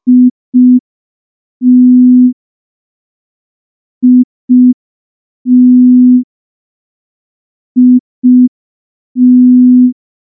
Morse code, heard as a keyed tone, about 5 wpm, 250 Hz. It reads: IT IT IT
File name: morse / morse-training-05WPM